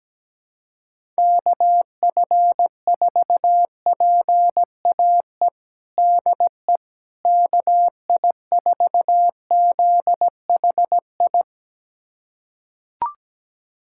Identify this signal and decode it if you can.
Morse code, heard as a keyed tone, about 17 wpm, 700 Hz. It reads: KF4PAE DE KI4ZHI
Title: USA Callsign DE Callsign - 6 Characters - 17wpm